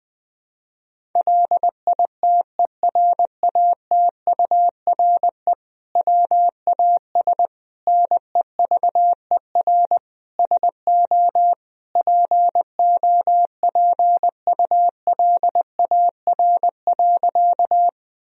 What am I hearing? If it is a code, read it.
Morse code, heard as a keyed tone, about 20 wpm, 700 Hz. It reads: LITERATURE WAS NEVER SO POPULAR.
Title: Sentences from Top 1000 Words - Review Part 3 - 20wpm